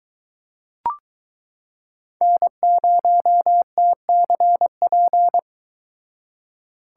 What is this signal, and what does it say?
Morse code, heard as a keyed tone, about 23 wpm, 700 Hz. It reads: N0TCP